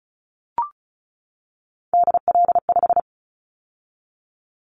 Morse code, heard as a keyed tone, about 34 wpm, 700 Hz. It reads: DL5
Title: Callsign Prefixes for Top 25 DXCC - 35wpm